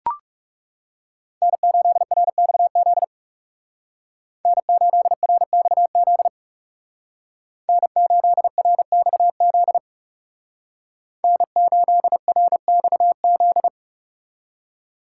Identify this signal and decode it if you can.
Morse code, heard as a keyed tone, about 37 wpm, 700 Hz. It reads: N8RXZ N8RXZ N8RXZ N8RXZ
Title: Speed Racing - USA 1x3 Call Signs 30wpm